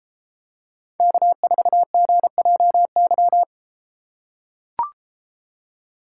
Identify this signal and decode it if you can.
Morse code, heard as a keyed tone, about 33 wpm, 700 Hz. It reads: K4GJY